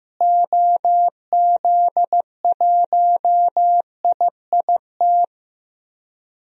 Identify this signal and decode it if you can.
Morse code, heard as a keyed tone, about 15 wpm, 700 Hz. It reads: OZ1IIT
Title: DXCC-100 Call Signs - 6 Characters - Repeated 3x - 15wpm